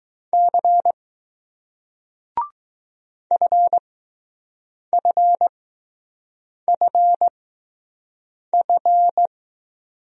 Morse code, heard as a keyed tone, about 20 wpm, spaced as for 6 wpm, 700 Hz. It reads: C FFFF